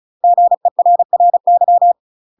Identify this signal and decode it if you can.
Morse code, heard as a keyed tone, about 34 wpm, 700 Hz. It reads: GERRY
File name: UK MASTER QSOs TRAINING / 35WPM / UK MASTER QSOs 0039